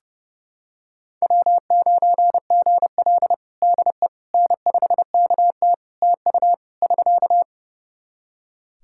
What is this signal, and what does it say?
Morse code, heard as a keyed tone, about 32 wpm, 700 Hz. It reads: W9GL DE N5KT TU <SK>
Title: Practice QSO - Exchange 4 - Ending - 30wpm